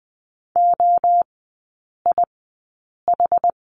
Morse code, heard as a keyed tone, about 20 wpm, spaced as for 8 wpm, 690 Hz. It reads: OIH
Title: 159 Lesson 26e - Letter g - Three Characters 20wpm